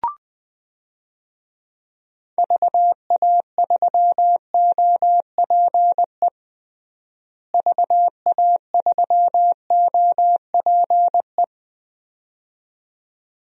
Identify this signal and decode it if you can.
Morse code, heard as a keyed tone, about 20 wpm, 700 Hz. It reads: VA3OPE VA3OPE
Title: DXCC-100 Call Signs - 6 Characters - Repeated 2x - 20wpm